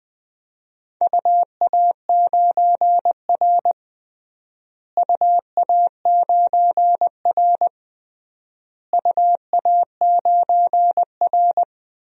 Morse code, heard as a keyed tone, about 20 wpm, 700 Hz. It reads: UA9R UA9R UA9R
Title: DXCC-100 Call Signs - 4 Characters - Repeated 3x - 20wpm